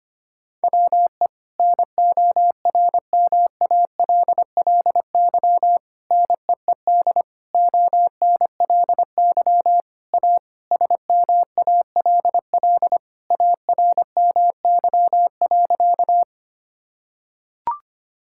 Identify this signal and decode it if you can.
Morse code, heard as a keyed tone, about 25 wpm, 700 Hz. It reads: WE NORMALLY NEED ONLY A SMALL ARMY.
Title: Sentences from Top 1000 Words - Review Part 6 - 25wpm